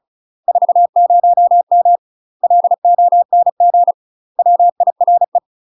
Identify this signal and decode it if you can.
Morse code, heard as a keyed tone, about 36 wpm, 700 Hz. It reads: 40M LONG WIRE
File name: UK MASTER QSOs TRAINING / 35WPM / UK MASTER QSOs 0041